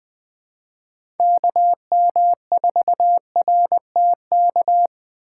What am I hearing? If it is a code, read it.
Morse code, heard as a keyed tone, about 20 wpm, 700 Hz. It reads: KM4RTK
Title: USA Call Signs - 3x Character Spacing - 20wpm